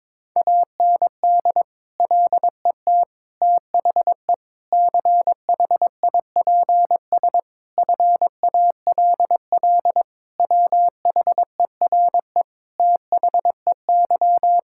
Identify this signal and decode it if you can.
Morse code, heard as a keyed tone, about 22 wpm, 700 Hz. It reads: AND LET THE CHIPS FALL WHERE THEY
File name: practice / 22wpm / Sentences with 8-Word Phrases - 22wpm